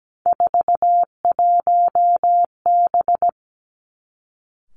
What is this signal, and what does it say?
Morse code, heard as a keyed tone, about 17 wpm, 700 Hz. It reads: 41B
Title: Three Challenging Letter-Number Combinations 17wpm